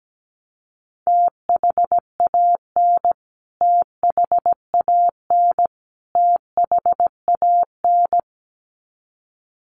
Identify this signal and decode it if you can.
Morse code, heard as a keyed tone, about 17 wpm, 700 Hz. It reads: THAN THAN THAN
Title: Top 200 Words Repeated 17wpm